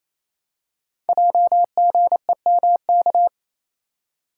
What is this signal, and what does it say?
Morse code, heard as a keyed tone, about 27 wpm, 700 Hz. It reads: JGEMK
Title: Code Groups - 5 Letters - 28wpm